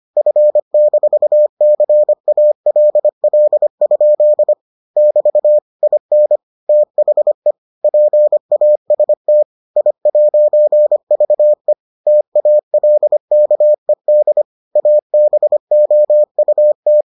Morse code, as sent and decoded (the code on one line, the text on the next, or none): ..-. -....- -.-. .- .-.. .-.. ..--.. / -...- / .. -. / - .... / .--. .- ... - / .. .----. ...- . / - .- .-.. -.- . -.. / .- -... --- ..- -
F-CALL? <BT> IN THE PAST I'VE TALKED ABOUT